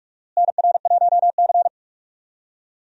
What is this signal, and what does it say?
Morse code, heard as a keyed tone, about 46 wpm, 700 Hz. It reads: NR1C
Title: USA Call Signs - 4 Characters - Repeated 3x - 45wpm